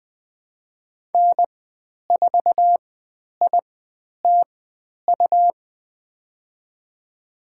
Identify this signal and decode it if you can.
Morse code, heard as a keyed tone, about 20 wpm, spaced as for 10 wpm, 700 Hz. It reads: N4ITU
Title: 102 Lesson 17g - Letter u - Call Signs 20wpm